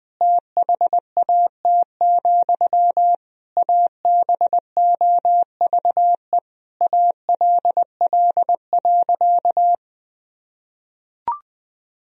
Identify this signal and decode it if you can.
Morse code, heard as a keyed tone, about 20 wpm, 700 Hz. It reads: THAT, ABOVE ALL.
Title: Sentences from Top 1000 Words - Review Part 2 - 20wpm